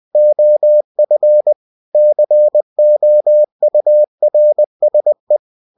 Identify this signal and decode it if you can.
Morse code, heard as a keyed tone, about 20 wpm, 600 Hz. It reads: OF COURSE
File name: cw-books / winnie-the-pooh / winnie-20wpm-page-0108